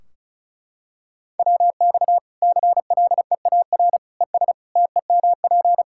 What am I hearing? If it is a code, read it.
Morse code, heard as a keyed tone, about 36 wpm, 700 Hz. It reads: WX CLEAR ES TEMP
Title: K1USN SST QSO Practice 35wpm